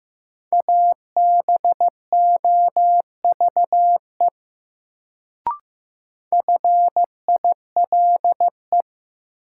Morse code, heard as a keyed tone, about 15 wpm, 700 Hz. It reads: ABOVE FILE
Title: Sentences from Top 1200 Words - New Word Drill Alt - 15wpm